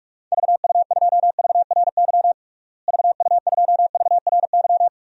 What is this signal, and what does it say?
Morse code, heard as a keyed tone, about 42 wpm, 700 Hz. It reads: VU2VRY VU2VRY
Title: DXCC-100 Call Signs - 6 Characters - Repeated 2x - 45wpm